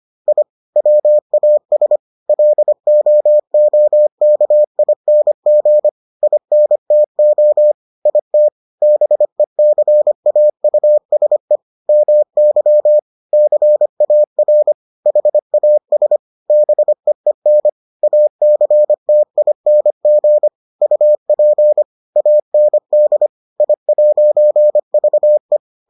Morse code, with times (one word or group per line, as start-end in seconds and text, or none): 0.3-0.4 I
0.8-2.0 WAS
2.3-5.9 LOOKING
6.2-7.7 INTO
8.1-8.5 IT
8.8-11.6 BECAUSE
11.9-13.0 MY
13.3-14.7 CAR
15.1-16.2 HAS
16.5-17.7 BEEN
18.0-20.5 ACTING
20.8-21.8 UP
22.2-23.3 AND
23.6-25.6 I'VE